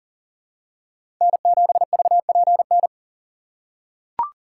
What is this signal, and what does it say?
Morse code, heard as a keyed tone, about 39 wpm, 700 Hz. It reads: N7VPN
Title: USA Call Signs - 5 Characters - 6x Character Spacing - 40wpm